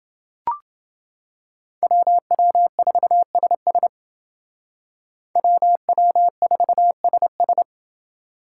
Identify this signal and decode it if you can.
Morse code, heard as a keyed tone, about 29 wpm, 700 Hz. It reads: WW4SS WW4SS